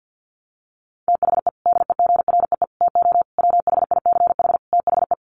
Morse code, heard as a keyed tone, about 50 wpm, 700 Hz. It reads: THE DEGREE TO WHICH THE
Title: Sets of 5 Words Encore2 50wpm